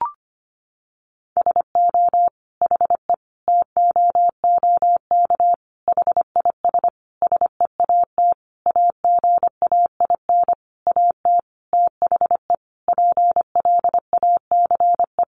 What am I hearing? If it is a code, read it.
Morse code, heard as a keyed tone, about 25 wpm, 700 Hz. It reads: SO HE TOOK HIS SEAT AGAIN AT THE PLACE
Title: Sentences from Top 1500 Words 25wpm